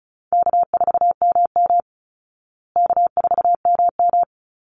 Morse code, heard as a keyed tone, about 34 wpm, 700 Hz. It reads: K4MM K4MM
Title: Easy Call Signs Repeated 2x - 35wpm